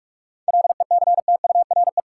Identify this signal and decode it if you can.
Morse code, heard as a keyed tone, about 47 wpm, 700 Hz. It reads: LECTURE
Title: Seven Letter Words - 45wpm